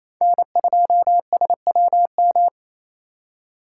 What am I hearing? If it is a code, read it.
Morse code, heard as a keyed tone, about 28 wpm, 700 Hz. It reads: N2SWM
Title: USA Call Signs - 3x Character Spacing - 28wpm